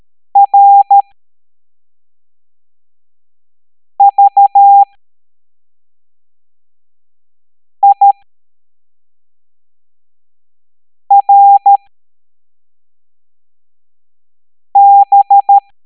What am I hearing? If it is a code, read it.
Morse code, heard as a keyed tone, about 13 wpm, spaced as for 3 wpm, 800 Hz. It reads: RVIRB